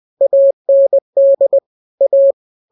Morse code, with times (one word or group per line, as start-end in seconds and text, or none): 0.2-1.6 AND
2.0-2.3 A